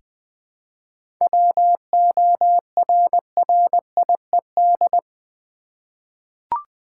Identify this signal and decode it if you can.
Morse code, heard as a keyed tone, about 20 wpm, 700 Hz. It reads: WORRIED
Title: Seven Letter Words - 20wpm